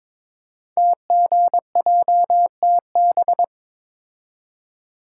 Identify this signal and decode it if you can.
Morse code, heard as a keyed tone, about 22 wpm, 700 Hz. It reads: TGJTB
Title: Code Groups - 5 Letters - 22wpm